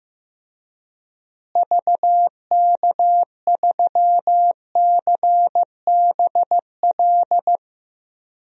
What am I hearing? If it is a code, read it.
Morse code, heard as a keyed tone, about 15 wpm, 700 Hz. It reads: VK3CBL
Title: DXCC-100 Call Signs - 6 Characters - Repeated 2x - 15wpm